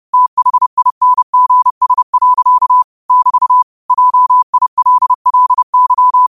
Morse code, heard as a keyed tone, about 30 wpm, 1000 Hz. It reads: THINGS. <BT> JILLY
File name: qotdfiles / 30wpm